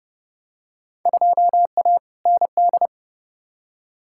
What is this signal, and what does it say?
Morse code, heard as a keyed tone, about 31 wpm, 700 Hz. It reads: 2A ND
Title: Field Day Exchanges - 30wpm